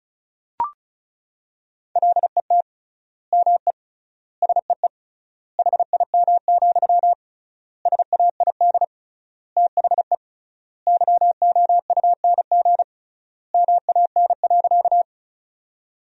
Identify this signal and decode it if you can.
Morse code, heard as a keyed tone, about 36 wpm, 700 Hz. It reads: LET ME SEE HIM, SAID THE YOUNG MAN.
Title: 3x Word-Spacing - Sentences from Top 400 Words - 35wpm